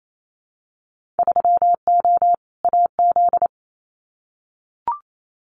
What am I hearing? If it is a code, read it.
Morse code, heard as a keyed tone, about 28 wpm, 700 Hz. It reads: 3O AZ
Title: Winter Field Day Exchanges 28wpm